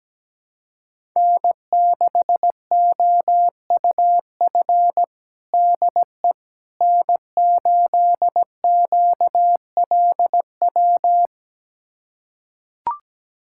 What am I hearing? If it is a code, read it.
Morse code, heard as a keyed tone, about 17 wpm, 700 Hz. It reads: N6OUF DE N8QLW
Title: USA Callsign DE Callsign - 5 Characters - 17wpm